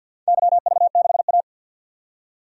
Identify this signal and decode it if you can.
Morse code, heard as a keyed tone, about 48 wpm, 700 Hz. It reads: YV6A
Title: DXCC-100 Call Signs - 4 Characters - 50wpm